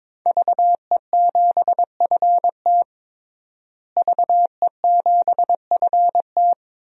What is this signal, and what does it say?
Morse code, heard as a keyed tone, about 22 wpm, 700 Hz. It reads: VE7FT VE7FT
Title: DXCC-100 Call Signs - 5 Characters - Repeated 2x - 22wpm